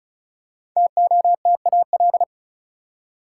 Morse code, heard as a keyed tone, about 34 wpm, 700 Hz. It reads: TOTAL
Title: Five Letter Words - 35wpm